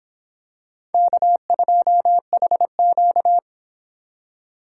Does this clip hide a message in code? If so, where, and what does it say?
Yes: Morse code from 0.9 to 3.4 s, heard as a keyed tone, about 26 wpm, 700 Hz. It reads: K2HQ